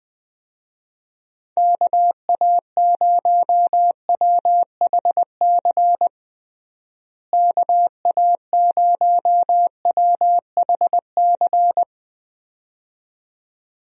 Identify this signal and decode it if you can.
Morse code, heard as a keyed tone, about 20 wpm, 700 Hz. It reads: KA0WHC KA0WHC
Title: USA Call Signs - 6 Characters - Repeated 2x - 20wpm